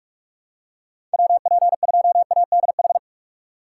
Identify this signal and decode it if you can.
Morse code, heard as a keyed tone, about 43 wpm, 700 Hz. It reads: WP2ADH
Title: USA Call Signs - 6 Characters - 3x Character Spacing - 45wpm